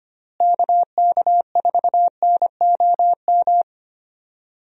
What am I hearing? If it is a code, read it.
Morse code, heard as a keyed tone, about 25 wpm, 700 Hz. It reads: KK4NOM
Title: USA Call Signs - 6 Characters - Repeated 3x - 25wpm